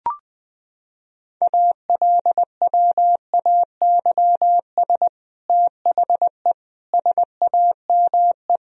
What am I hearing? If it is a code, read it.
Morse code, heard as a keyed tone, about 20 wpm, 700 Hz. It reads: ALWAYS THE SAME
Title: Sentences from Top 1000 Words - Review Part 1 - 20wpm